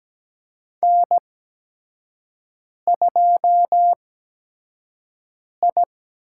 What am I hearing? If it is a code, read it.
Morse code, heard as a keyed tone, about 17 wpm, spaced as for 5 wpm, 700 Hz. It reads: N2I
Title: USA Call Signs - 8x Character Spacing - 17wpm